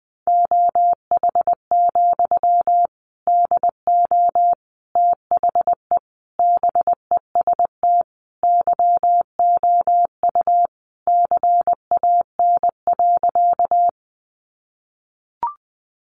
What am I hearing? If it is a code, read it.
Morse code, heard as a keyed tone, about 20 wpm, 700 Hz. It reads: OH, DO THE BEST YOU CAN.
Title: Sentences from Top 500 Words 20wpm